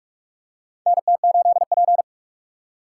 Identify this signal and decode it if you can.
Morse code, heard as a keyed tone, about 46 wpm, 700 Hz. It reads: NT8P
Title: USA Call Signs - 4x Character Spacing - 45wpm